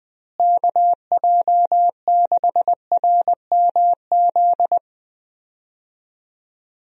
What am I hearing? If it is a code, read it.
Morse code, heard as a keyed tone, about 21 wpm, 700 Hz. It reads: KJ6RMZ
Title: USA Call Signs - 6 Characters - Repeated 3x - 20wpm